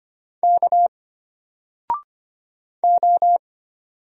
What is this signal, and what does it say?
Morse code, heard as a keyed tone, about 25 wpm, 700 Hz. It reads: K O
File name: Single-Letter-Number-25wpm